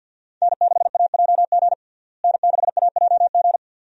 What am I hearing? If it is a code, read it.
Morse code, heard as a keyed tone, about 51 wpm, 700 Hz. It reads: N6AJG N6AJG
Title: USA Call Signs - 5 Characters - Repeated 2x - 50wpm